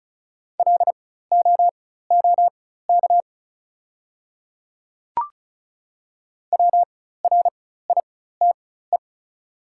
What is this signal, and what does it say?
Morse code, heard as a keyed tone, about 34 wpm, spaced as for 16 wpm, 690 Hz. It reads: LOOK WRITE
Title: 4x Character Spacing - Top 100 Words - 35wpm